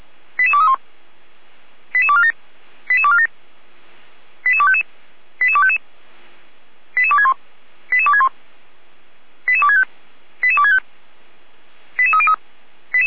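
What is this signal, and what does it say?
Five-tone selective call (ZVEI), heard as quick runs of five tones, tones of about 70 ms. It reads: 88231 88247 88247 88255 88255 88261 88261 88265 88265 88303
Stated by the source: Five-tone ZVEI selective call 88213, received by SDR.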